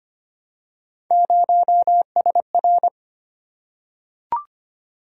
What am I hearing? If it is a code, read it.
Morse code, heard as a keyed tone, about 25 wpm, 700 Hz. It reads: 0SR
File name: Three Letter-Number Combinations - 25wpm